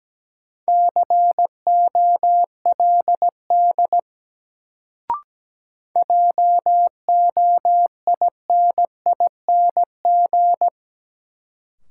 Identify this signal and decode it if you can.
Morse code, heard as a keyed tone, about 17 wpm, 700 Hz. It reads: COLD JOINING
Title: Sets of 5 Words - New Word Drill - 17wpm